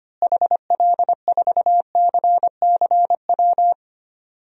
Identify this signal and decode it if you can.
Morse code, heard as a keyed tone, about 25 wpm, 700 Hz. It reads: HL4CCW
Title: DXCC-100 Call Signs - 6 Characters - Repeated 3x - 25wpm